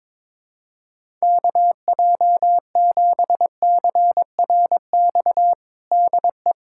Morse code, heard as a keyed tone, about 22 wpm, 700 Hz. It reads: KJ7CRX DE
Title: USA Callsign DE Callsign 22wpm